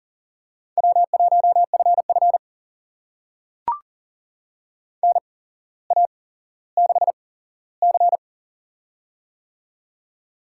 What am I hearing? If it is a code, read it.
Morse code, heard as a keyed tone, about 39 wpm, spaced as for 17 wpm, 700 Hz. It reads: W1FF NA6C
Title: USA Call Signs - 4 Characters - 8x Character Spacing - 40wpm